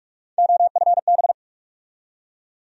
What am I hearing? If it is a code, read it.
Morse code, heard as a keyed tone, about 42 wpm, 700 Hz. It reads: OFB
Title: USA Call Sign Postfixes - 3 Letters 45wpm